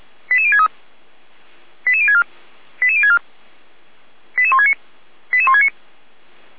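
Five-tone selective call (ZVEI), heard as quick runs of five tones, tones of about 70 ms. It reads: 88063 88064 88064 88169 88169
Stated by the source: ZVEI five-tone selective call received on SDR.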